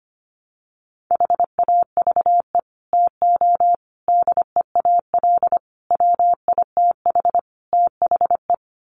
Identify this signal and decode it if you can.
Morse code, heard as a keyed tone, about 25 wpm, 700 Hz. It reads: HAVE TO DEAL WITH THE
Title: Sets of 5 Words Encore2 25wpm